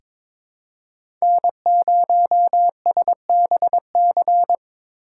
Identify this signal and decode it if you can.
Morse code, heard as a keyed tone, about 22 wpm, 700 Hz. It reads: N0SBC
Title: USA Call Signs - 5 Characters - 8x Character Spacing - 22wpm